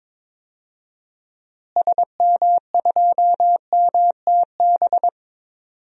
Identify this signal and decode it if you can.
Morse code, heard as a keyed tone, about 22 wpm, 700 Hz. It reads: SM2MTB
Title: DXCC-100 Call Signs - 6 Characters - Repeated 2x - 22wpm